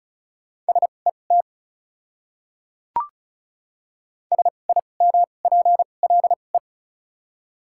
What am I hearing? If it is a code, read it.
Morse code, heard as a keyed tone, about 36 wpm, spaced as for 25 wpm, 700 Hz. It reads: SET SIMPLE
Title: 2x Character Spacing - Top 500 Words - 35wpm